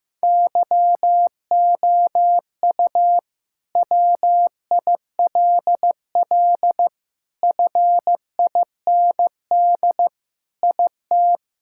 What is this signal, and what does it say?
Morse code, heard as a keyed tone, about 15 wpm, 700 Hz. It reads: YOU WILL FIND IT
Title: Sentences from Top 1600 Words 15wpm